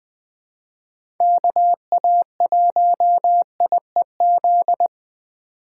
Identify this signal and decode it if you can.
Morse code, heard as a keyed tone, about 20 wpm, 700 Hz. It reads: KA1IEZ
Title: USA Call Signs - 3x Character Spacing - 20wpm